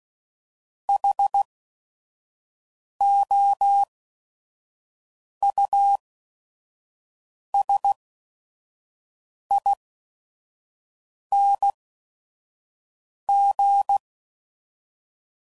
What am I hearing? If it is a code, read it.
Morse code, heard as a keyed tone, about 16 wpm, spaced as for 5 wpm, 800 Hz. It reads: HOUSING
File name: Suffixes / Special ING's / VerySlowToFaster ING's